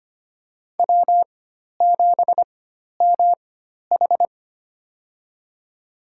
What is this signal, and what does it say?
Morse code, heard as a keyed tone, about 25 wpm, spaced as for 12 wpm, 700 Hz. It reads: W7MH
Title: USA Call Signs - 4 Characters - 4x Character Spacing - 25wpm